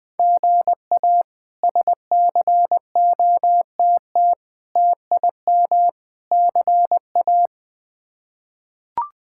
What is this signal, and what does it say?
Morse code, heard as a keyed tone, about 20 wpm, 700 Hz. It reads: GA SCOTT TIM CA
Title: K1USN SST QSO Practice - 20wpm